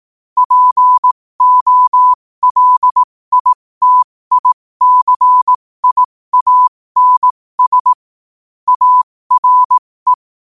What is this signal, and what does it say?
Morse code, heard as a keyed tone, about 18 wpm, spaced as for 15 wpm, 1000 Hz. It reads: POLITICIANS ARE
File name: qotdfiles / 15wpm